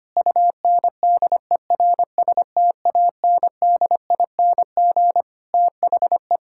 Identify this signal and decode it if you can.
Morse code, heard as a keyed tone, about 25 wpm, 700 Hz. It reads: UNDERSTANDING THE
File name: practice / 25wpm / Sentences with 9-Word Phrases - 25wpm